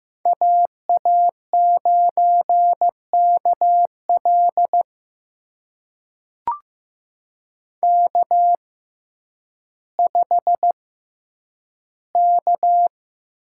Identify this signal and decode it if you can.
Morse code, heard as a keyed tone, about 15 wpm, spaced as for 9 wpm, 700 Hz. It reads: AA9KL K5K
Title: USA Call Signs - 5 Characters - 6x Character Spacing - 15wpm